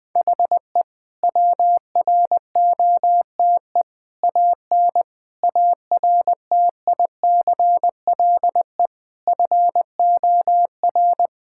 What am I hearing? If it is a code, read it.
Morse code, heard as a keyed tone, about 20 wpm, 700 Hz. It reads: HE WROTE AN ARTICLE FOR
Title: Sentences with 5-Word Phrases - Spoken Beginning - 20wpm